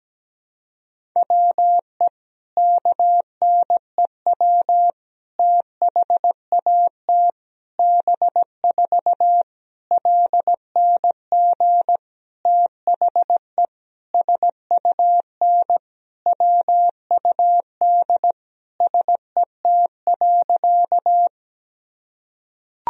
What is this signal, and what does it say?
Morse code, heard as a keyed tone, about 17 wpm, 700 Hz. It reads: WE KNEW THAT B4 LNG THE SUN WUD SET.
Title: Sentences from Top 400 Words and Abbreviations 17wpm